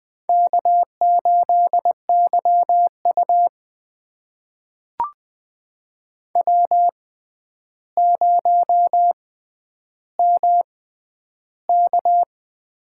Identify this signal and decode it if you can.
Morse code, heard as a keyed tone, about 20 wpm, spaced as for 10 wpm, 700 Hz. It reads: K8YU W0MK
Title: USA Call Signs - 4 Characters - 6x Character Spacing - 20wpm